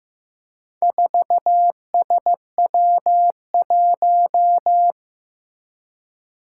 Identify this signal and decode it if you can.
Morse code, heard as a keyed tone, about 16 wpm, 700 Hz. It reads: 4SW1
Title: Four Challenging Letter-Number Combinations 15wpm